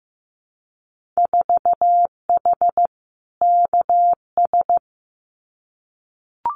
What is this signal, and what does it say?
Morse code, heard as a keyed tone, about 15 wpm, 700 Hz. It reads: 4H KS
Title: Winter Field Day Exchanges Encore 15wpm